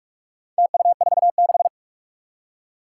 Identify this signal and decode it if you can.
Morse code, heard as a keyed tone, about 44 wpm, 700 Hz. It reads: TU46
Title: Four Challenging Letter-Number Combinations 45wpm